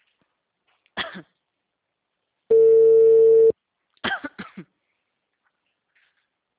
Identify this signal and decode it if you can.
Ringback tone (China, 450 Hz).
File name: out- -2003- - - .36536